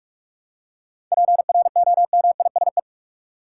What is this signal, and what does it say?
Morse code, heard as a keyed tone, about 43 wpm, 700 Hz. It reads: PROMISE
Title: Seven Letter Words - 45wpm